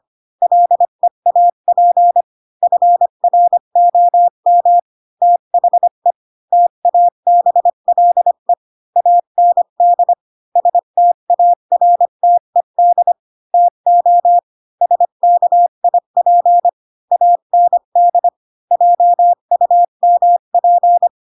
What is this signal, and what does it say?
Morse code, heard as a keyed tone, about 25 wpm, 700 Hz. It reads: LEAP FROM THE TABLE AND STARTED TO SKIP AND JUMP
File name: Pinocchio / Pinocchio ch8 20 25 ik1hge